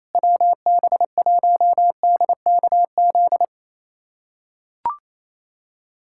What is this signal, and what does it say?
Morse code, heard as a keyed tone, about 27 wpm, 700 Hz. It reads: WB1DKZ
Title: USA Callsign DE Callsign 28wpm